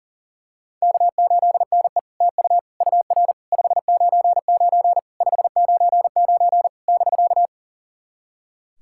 Morse code, heard as a keyed tone, about 40 wpm, 690 Hz. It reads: K8NE TU UR 599 599 <BK>
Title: SOTA QSO Practice 40wpm